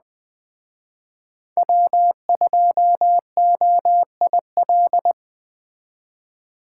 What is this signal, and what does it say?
Morse code, heard as a keyed tone, about 20 wpm, 700 Hz. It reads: W2OIL
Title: Speed Racing - USA 1x3 Call Signs 17wpm